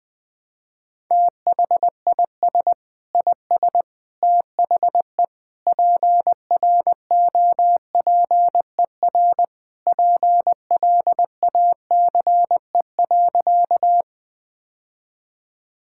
Morse code, heard as a keyed tone, about 20 wpm, 700 Hz. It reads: THIS IS THE PROPER PLACE.
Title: Sentences from Top 1000 Words - Review Part 5 - 20wpm